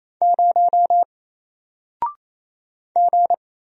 Morse code, heard as a keyed tone, about 27 wpm, 700 Hz. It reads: 0 G
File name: Single-Letter-Number-28wpm